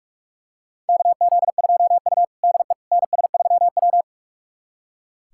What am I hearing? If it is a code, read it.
Morse code, heard as a keyed tone, about 43 wpm, 700 Hz. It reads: KZ2U DE NS3W
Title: USA Callsign DE Callsign - 4 Characters - 45wpm